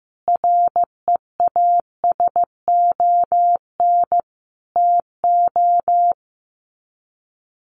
Morse code, heard as a keyed tone, about 15 wpm, 700 Hz. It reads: REASON TO
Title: Sets of 5 Words Encore 15wpm